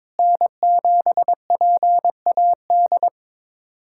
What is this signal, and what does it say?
Morse code, heard as a keyed tone, about 22 wpm, 700 Hz. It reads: N7PAD
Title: USA Callsign DE Callsign - 5 Characters - 22wpm